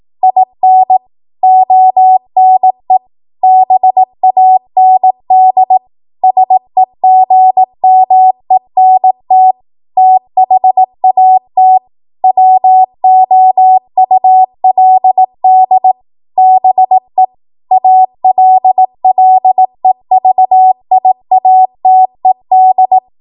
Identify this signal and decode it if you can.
Morse code, heard as a keyed tone, about 18 wpm, 750 Hz. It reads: IN ONE BAND SEGMENT THAT WOULD BE ALLEVIATED
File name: Morse / 18WPM